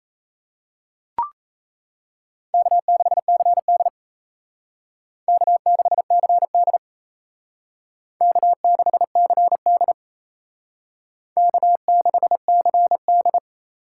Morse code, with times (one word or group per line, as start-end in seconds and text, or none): 2.5-3.9 K6CD
5.3-6.8 K6CD
8.2-9.9 K6CD
11.4-13.4 K6CD